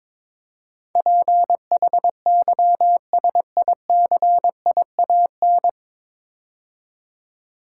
Morse code, heard as a keyed tone, about 23 wpm, 700 Hz. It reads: PHYSICIAN
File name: 9 Letter Words - First 3 Letters Given - 22wpm